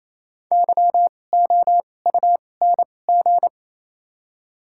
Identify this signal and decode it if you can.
Morse code, heard as a keyed tone, about 28 wpm, 690 Hz. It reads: Y O U N G
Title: 2x Character Spacing - Top 500 Words - 28wpm